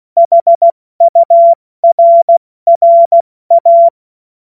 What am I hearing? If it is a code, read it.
Morse code, heard as a keyed tone, about 16 wpm, spaced as for 14 wpm, 680 Hz. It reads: HURRA
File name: Morsekurs / Witze / Witz 10. August 2021, 70 F80